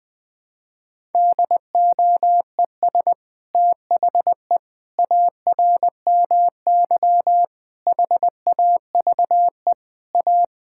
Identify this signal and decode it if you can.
Morse code, heard as a keyed tone, about 21 wpm, 700 Hz. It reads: DOES THE ARMY HAVE A
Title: Sentences from Top 1000 Words - Review Part 5 - 20wpm